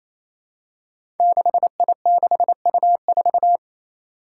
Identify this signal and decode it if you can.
Morse code, heard as a keyed tone, about 28 wpm, 700 Hz. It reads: 6I6U4
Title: Five Challenging Letter-Number Combinations 28wpm